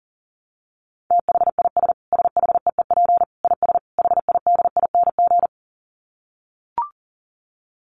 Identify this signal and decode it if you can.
Morse code, heard as a keyed tone, about 40 wpm, 700 Hz. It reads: THIS SHEEP IS HIDING
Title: Dit Challenge 40wpm